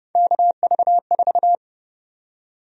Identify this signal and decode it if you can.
Morse code, heard as a keyed tone, about 30 wpm, 700 Hz. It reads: KV4
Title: USA Call Sign Prefixes - 30wpm